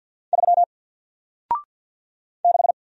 Morse code, heard as a keyed tone, about 54 wpm, 700 Hz. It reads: 3 6
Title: Single Letter-Number Alt 50wpm